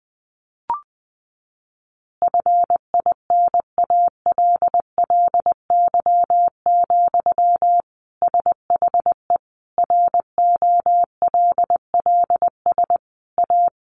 Morse code, heard as a keyed tone, about 20 wpm, 700 Hz. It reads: FINALLY, SHE ROLLS A
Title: Easy Phrases 20wpm